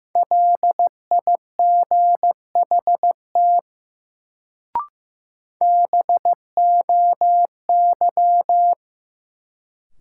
Morse code, heard as a keyed tone, about 15 wpm, 700 Hz. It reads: LIGHT BOY
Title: Sentences from Top 200 Words - New Word Drill - 15wpm